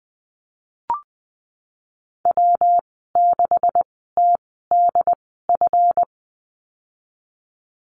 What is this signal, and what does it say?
Morse code, heard as a keyed tone, about 20 wpm, spaced as for 15 wpm, 700 Hz. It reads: W6TDF
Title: 179 Lesson 29g - Slash - Call Signs 20wpm